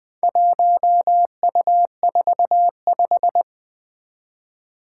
Morse code, heard as a keyed tone, about 20 wpm, 700 Hz. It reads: 1U45
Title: Four Challenging Letter-Number Combinations 20wpm